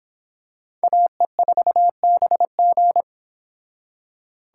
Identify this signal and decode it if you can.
Morse code, heard as a keyed tone, about 25 wpm, 700 Hz. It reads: AE4BG